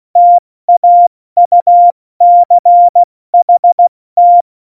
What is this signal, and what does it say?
Morse code, heard as a keyed tone, about 16 wpm, 700 Hz. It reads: TAUCHT
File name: Morsekurs / Witze / Witz 08. Dezember 2020, 70 F80